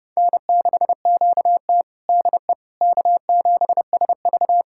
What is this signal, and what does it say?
Morse code, heard as a keyed tone, about 30 wpm, 700 Hz. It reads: N6QT DE K7SV
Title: USA Callsign DE Callsign - 4 Characters - 30wpm